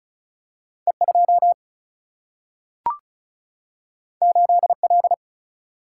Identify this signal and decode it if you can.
Morse code, heard as a keyed tone, about 36 wpm, 700 Hz. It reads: E 2 8 L
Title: Single-Letter Single-Number 35wpm